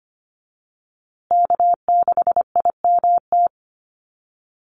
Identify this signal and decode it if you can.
Morse code, heard as a keyed tone, about 25 wpm, 700 Hz. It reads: K6IMT
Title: USA Call Signs Repeated 3x 25wpm